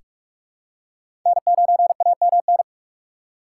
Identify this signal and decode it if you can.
Morse code, heard as a keyed tone, about 42 wpm, 700 Hz. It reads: N9AMN
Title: USA Call Signs - 6x Character Spacing - 45wpm